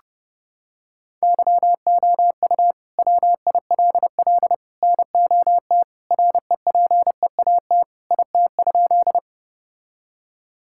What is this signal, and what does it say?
Morse code, heard as a keyed tone, about 30 wpm, 700 Hz. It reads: YOU WILL NOT REPEAT IT?
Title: Sentences from Top 1600 Words 30wpm